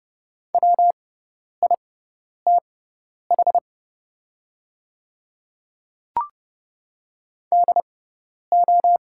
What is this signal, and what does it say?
Morse code, heard as a keyed tone, about 28 wpm, spaced as for 10 wpm, 700 Hz. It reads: WITH DO
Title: 6x Character Spacing - Top 100 Words - 30wpm